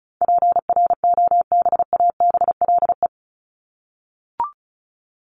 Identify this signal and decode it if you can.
Morse code, heard as a keyed tone, about 34 wpm, 700 Hz. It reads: PROBABLE
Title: LY Suffix - Top 100 Words with Base Words 35wpm